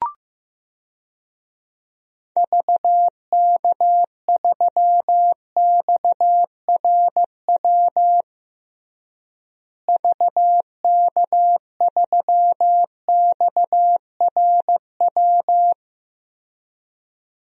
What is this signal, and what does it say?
Morse code, heard as a keyed tone, about 16 wpm, 700 Hz. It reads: VK3XRW VK3XRW
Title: DXCC-100 Call Signs - 6 Characters - Repeated 2x - 15wpm